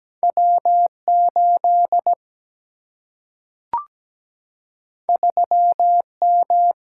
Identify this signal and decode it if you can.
Morse code, heard as a keyed tone, about 17 wpm, 700 Hz. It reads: W8 3M
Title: Single-Letter Single-Number 17wpm